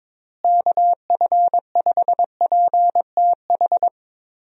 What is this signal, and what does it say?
Morse code, heard as a keyed tone, about 22 wpm, 700 Hz. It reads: KF5PTH
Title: USA Call Signs - 6 Characters - Repeated 2x - 22wpm